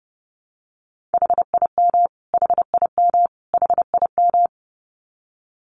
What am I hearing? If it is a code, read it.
Morse code, heard as a keyed tone, about 30 wpm, 700 Hz. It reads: HIM HIM HIM
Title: Top 300 Words Repeated 30wpm